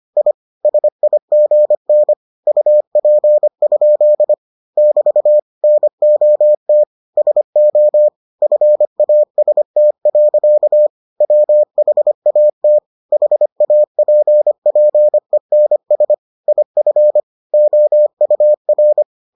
Morse code, heard as a keyed tone, about 25 wpm, 600 Hz. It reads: I SIGN UP? <BT> NOT SO FAST. WHAT HAPPENS IF OUR